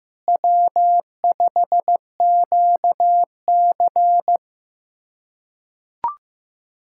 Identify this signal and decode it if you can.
Morse code, heard as a keyed tone, about 15 wpm, 700 Hz. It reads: W5QC
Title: USA Callsign DE Callsign - 4 Characters - 15wpm